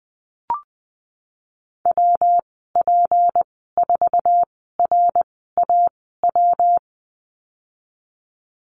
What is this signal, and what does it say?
Morse code, heard as a keyed tone, about 20 wpm, spaced as for 15 wpm, 700 Hz. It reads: WP4RAW
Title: 179 Lesson 29g - Slash - Call Signs 20wpm